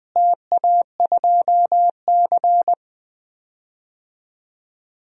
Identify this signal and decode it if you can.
Morse code, heard as a keyed tone, about 20 wpm, 700 Hz. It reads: TA2C
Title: DXCC-100 Call Signs - 4 Characters - Repeated 3x - 20wpm